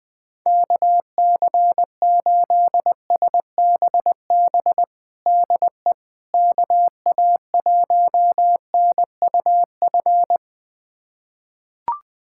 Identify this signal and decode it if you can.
Morse code, heard as a keyed tone, about 20 wpm, 700 Hz. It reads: KC8SBB DE KA1NUF
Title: USA Callsign DE Callsign - 6 Characters - 20wpm